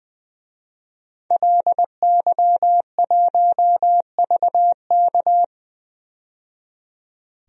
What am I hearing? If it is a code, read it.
Morse code, heard as a keyed tone, about 20 wpm, 700 Hz. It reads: LY1VK
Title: DXCC-100 Call Signs - 5 Characters - 20wpm